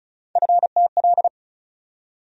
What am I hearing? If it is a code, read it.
Morse code, heard as a keyed tone, about 36 wpm, 700 Hz. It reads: FTL
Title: USA Call Sign Postfixes - 3 Letters 35wpm